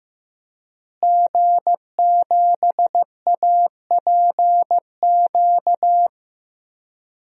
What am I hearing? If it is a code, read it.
Morse code, heard as a keyed tone, about 15 wpm, 700 Hz. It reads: G7APQ
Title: DXCC-100 Call Signs - 5 Characters - Repeated 3x - 15wpm